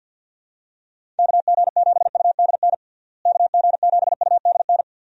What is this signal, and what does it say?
Morse code, heard as a keyed tone, about 52 wpm, 700 Hz. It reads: KG7UDN KG7UDN
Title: USA Call Signs - 6 Characters - Repeated 3x - 50wpm